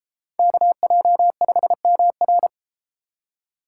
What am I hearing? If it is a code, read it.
Morse code, heard as a keyed tone, about 30 wpm, 700 Hz. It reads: KJ5MR